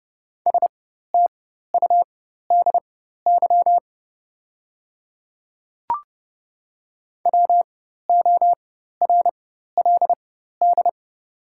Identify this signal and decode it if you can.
Morse code, heard as a keyed tone, about 30 wpm, spaced as for 14 wpm, 700 Hz. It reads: STUDY WORLD
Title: 4x Character Spacing - Top 300 Words - 30wpm